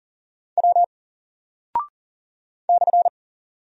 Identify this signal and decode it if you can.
Morse code, heard as a keyed tone, about 40 wpm, 700 Hz. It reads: W /
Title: Single Character (Letter:Number:Punctuation:Prosign) 40wpm